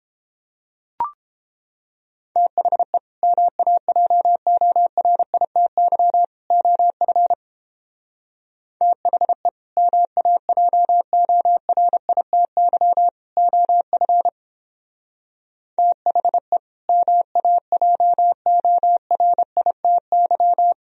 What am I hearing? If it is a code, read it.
Morse code, heard as a keyed tone, about 29 wpm, 700 Hz. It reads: THE MAJORITY OF THE MAJORITY OF THE MAJORITY